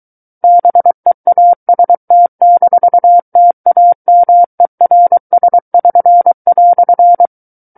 Morse code, heard as a keyed tone, about 23 wpm, 700 Hz. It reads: BEAST-TAMERS<SN>"
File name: Book One / 14-Chapter 14 - 23 WPM